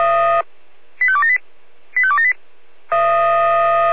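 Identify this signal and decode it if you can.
Five-tone selective call (ZVEI), heard as quick runs of five tones, tones of about 70 ms. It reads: 85278 85278
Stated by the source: ZVEI five-tone call received by SDR.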